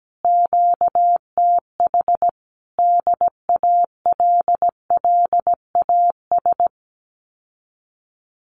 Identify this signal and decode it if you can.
Morse code, heard as a keyed tone, about 17 wpm, 700 Hz. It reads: QTH DALLAS
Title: Common QSO Phrases Encore 17wpm